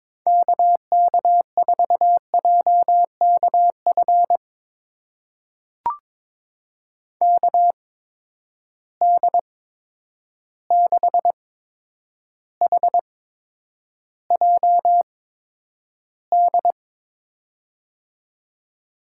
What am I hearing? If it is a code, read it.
Morse code, heard as a keyed tone, about 22 wpm, spaced as for 9 wpm, 700 Hz. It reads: KK4JKF KD6HJD
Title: USA Call Signs - 6 Characters - 8x Character Spacing - 22wpm